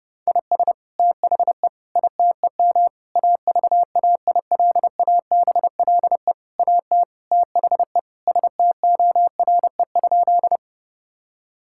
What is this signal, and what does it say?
Morse code, heard as a keyed tone, about 32 wpm, 700 Hz. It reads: IS THE ITEM AVAILABLE AT THE STORE?
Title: Sentences from Top 1000 Words - Review Part 6 - 30wpm